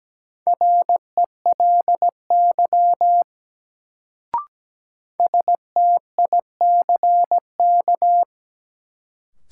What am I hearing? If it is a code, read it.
Morse code, heard as a keyed tone, about 17 wpm, 700 Hz. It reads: RELY STICK
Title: Sentences from Top 1200 Words - New Word Drill Alt - 17wpm